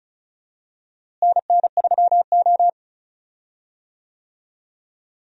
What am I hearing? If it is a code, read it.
Morse code, heard as a keyed tone, about 34 wpm, 700 Hz. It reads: NN3O
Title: USA Call Signs - 4 Characters - Repeated 3x - 35wpm